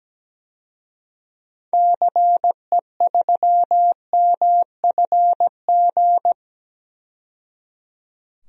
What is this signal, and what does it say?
Morse code, heard as a keyed tone, about 17 wpm, 700 Hz. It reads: CE3MFG
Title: DXCC-100 Call Signs - 6 Characters - 17wpm